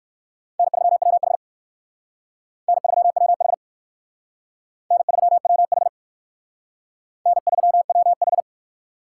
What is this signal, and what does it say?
Morse code, heard as a keyed tone, about 59 wpm, 700 Hz. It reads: N3WH N3WH N3WH N3WH